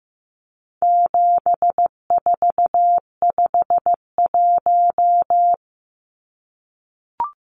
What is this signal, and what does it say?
Morse code, heard as a keyed tone, about 15 wpm, 700 Hz. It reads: 7451
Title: Numbers - 4-Digit 15wpm